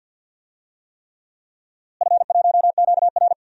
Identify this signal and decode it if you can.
Morse code, heard as a keyed tone, about 48 wpm, 700 Hz. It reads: F1QR
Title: DXCC-100 Call Signs - 4 Characters - Repeated 3x - 50wpm